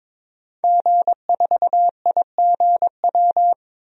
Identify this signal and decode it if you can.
Morse code, heard as a keyed tone, about 22 wpm, 700 Hz. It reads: G4IGW
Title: DXCC-100 Call Signs - 5 Characters - Repeated 2x - 22wpm